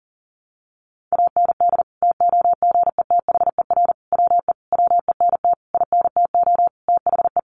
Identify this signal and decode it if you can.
Morse code, heard as a keyed tone, about 37 wpm, 700 Hz. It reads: AND TOGETHER WE WENT INTO THE
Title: Sentences from Top 600 Words 40wpm